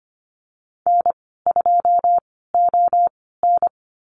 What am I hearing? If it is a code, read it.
Morse code, heard as a keyed tone, about 25 wpm, spaced as for 16 wpm, 700 Hz. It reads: N2ON
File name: Lesson 15g - Number 5 - Call Signs 25wpm